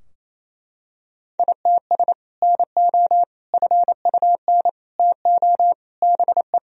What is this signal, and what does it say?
Morse code, heard as a keyed tone, about 28 wpm, 700 Hz. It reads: ITS NO FUN TO BE
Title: Sentences from Top 1000 Words - Review Part 4 - 28wpm